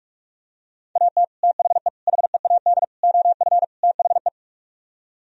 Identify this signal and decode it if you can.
Morse code, heard as a keyed tone, about 46 wpm, 700 Hz. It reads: AT THE HEAD OF THE